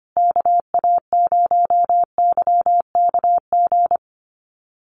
Morse code, heard as a keyed tone, about 25 wpm, 700 Hz. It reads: KA0YKG
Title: USA Call Signs Repeated 3x 25wpm